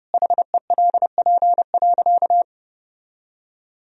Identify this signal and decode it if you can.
Morse code, heard as a keyed tone, about 30 wpm, 700 Hz. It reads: HELP.
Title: Sentences from Top 1000 Words - Review Part 3 - 30wpm